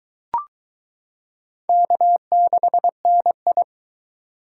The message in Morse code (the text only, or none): K6NI